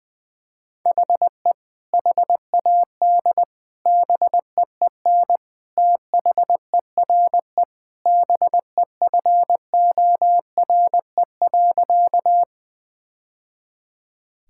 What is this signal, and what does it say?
Morse code, heard as a keyed tone, about 21 wpm, 700 Hz. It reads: HE HAD BEEN THERE BEFORE.
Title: Sentences with 3-Word Phrases - 20wpm